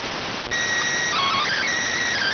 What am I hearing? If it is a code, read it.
SSTV (slow-scan television), PD 290.